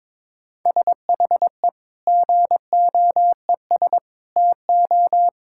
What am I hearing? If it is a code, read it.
Morse code, heard as a keyed tone, about 21 wpm, 700 Hz. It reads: SHE GOES TO
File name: Difficult Sentences with Hints - 22wpm